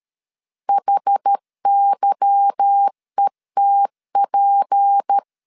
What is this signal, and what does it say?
Morse code, heard as a keyed tone, about 13 wpm, 800 Hz. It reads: HYETP
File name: LEZIONI / 10 SIGNS / 10A SLASH 12wpm 2314